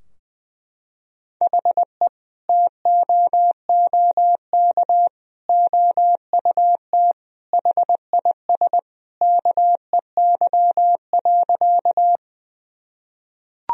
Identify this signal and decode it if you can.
Morse code, heard as a keyed tone, about 20 wpm, 700 Hz. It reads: HE TOOK OUT HIS KEY.
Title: Sentences from Top 1000 Words - Review Part 4 - 20wpm